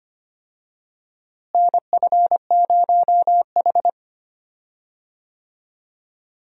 Morse code, heard as a keyed tone, about 25 wpm, 700 Hz. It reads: NF0H
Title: USA Call Signs - 4 Characters - Repeated 2x - 25wpm